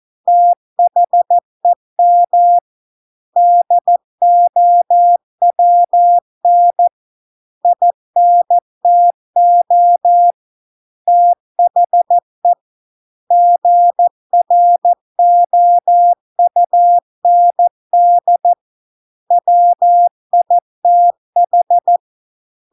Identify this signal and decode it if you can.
Morse code, heard as a keyed tone, about 14 wpm, 700 Hz. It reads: THEM DOWN INTO THE GROUND WITH